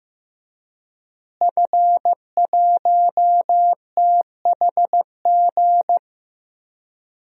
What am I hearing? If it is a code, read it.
Morse code, heard as a keyed tone, about 15 wpm, 700 Hz. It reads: F1THG
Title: DXCC-100 Call Signs - 5 Characters - Repeated 3x - 15wpm